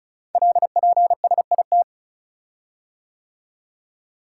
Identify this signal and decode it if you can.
Morse code, heard as a keyed tone, about 36 wpm, 690 Hz. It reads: LPSIT
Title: Code Groups - 5 Letters - 35wpm